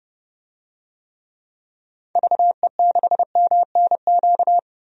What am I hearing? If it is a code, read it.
Morse code, heard as a keyed tone, about 31 wpm, 700 Hz. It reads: VE6MNQ
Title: DXCC-100 Call Signs - 6 Characters - 30wpm